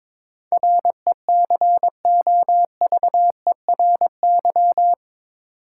Morse code, heard as a keyed tone, about 22 wpm, 700 Hz. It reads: RECOVERY
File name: practice / 22wpm / Eight Letter Words - 22wpm